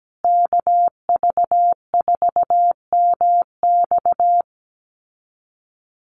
Morse code, heard as a keyed tone, about 17 wpm, 700 Hz. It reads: KV4MX
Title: USA Call Signs Repeated 3x 17wpm